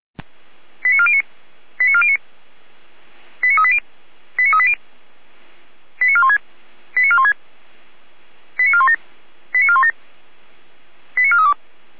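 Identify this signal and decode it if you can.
Five-tone selective call (ZVEI), heard as quick runs of five tones, tones of about 70 ms. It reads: 89409 89409 89380 89380 89416 89416 89417 89417 89432
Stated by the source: ZVEI five-tone selective call received on SDR.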